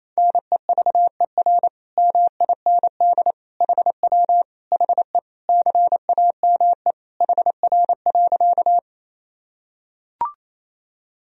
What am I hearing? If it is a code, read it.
Morse code, heard as a keyed tone, about 28 wpm, 700 Hz. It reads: NEVER MIND HW HE CAME HR.
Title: Sentences from Top 500 Words and Abbreviations 28wpm